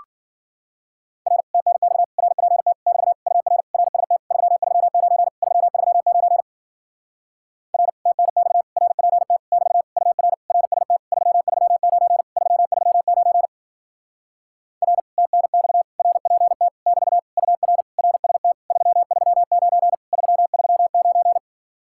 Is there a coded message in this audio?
Yes: Morse code from 1.3 to 21.4 s, heard as a keyed tone, about 52 wpm, 700 Hz. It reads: R TNX RPT <BT> UR RST 339 339 R TNX RPT <BT> UR RST 339 339 R TNX RPT <BT> UR RST 339 339